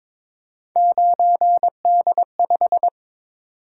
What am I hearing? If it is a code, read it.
Morse code, heard as a keyed tone, about 22 wpm, 700 Hz. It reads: 9D5
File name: practice / 22wpm / Three Letter-Number Combinations - 22wpm